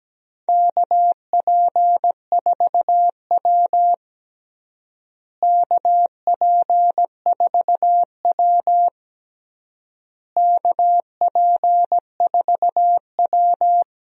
Morse code, heard as a keyed tone, about 17 wpm, 700 Hz. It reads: KP4W KP4W KP4W
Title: DXCC-100 Call Signs - 4 Characters - Repeated 3x - 17wpm